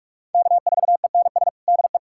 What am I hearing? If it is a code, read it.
Morse code, heard as a keyed tone, about 46 wpm, 700 Hz. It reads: K4ENS DE
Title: USA Callsign DE Callsign 45wpm